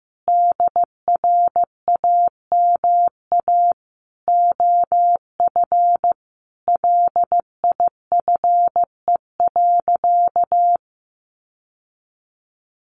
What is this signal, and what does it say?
Morse code, heard as a keyed tone, about 15 wpm, 700 Hz. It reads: DRAMA OF LIFE.
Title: Sentences from Top 1500 Words 15wpm